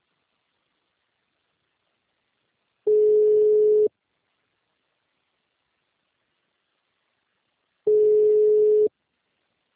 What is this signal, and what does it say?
Ringback tone (Europe, 425 Hz).